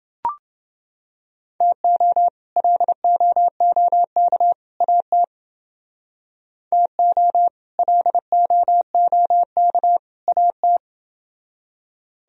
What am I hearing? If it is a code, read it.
Morse code, heard as a keyed tone, about 29 wpm, 700 Hz. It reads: TO LOOK AT TO LOOK AT